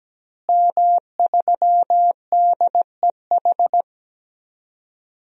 Morse code, heard as a keyed tone, about 17 wpm, 700 Hz. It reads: M3DEH
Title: DXCC-100 Call Signs - 5 Characters - Repeated 2x - 17wpm